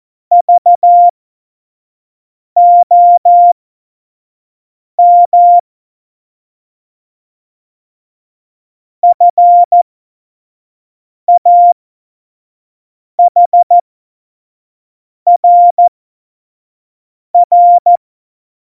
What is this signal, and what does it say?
Morse code, heard as a keyed tone, about 14 wpm, spaced as for 5 wpm, 700 Hz. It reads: VOM FAHRR
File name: Morsekurs / Witze / Witz 01. September 2020, 25 F70